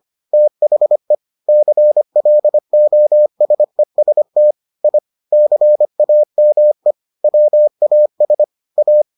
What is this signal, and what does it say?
Morse code, heard as a keyed tone, about 25 wpm, 600 Hz. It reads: THE CLOSEST I CAME WAS A